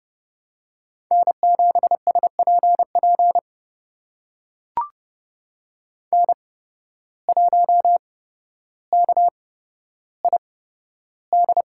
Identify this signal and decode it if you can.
Morse code, heard as a keyed tone, about 30 wpm, spaced as for 13 wpm, 700 Hz. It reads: N7SPP N1KID
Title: USA Call Signs - 5 Characters - 8x Character Spacing - 30wpm